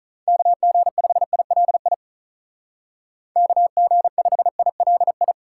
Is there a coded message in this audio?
Yes: Morse code from 0.3 to 5.3 s, heard as a keyed tone, about 45 wpm, 700 Hz. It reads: KG5ILI KG5ILI